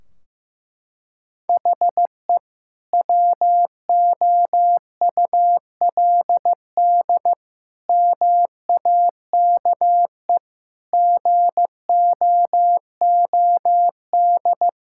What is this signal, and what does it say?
Morse code, heard as a keyed tone, about 14 wpm, 700 Hz. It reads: HE WOULD MAKE GOOD
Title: Sentences from Top 1600 Words 15wpm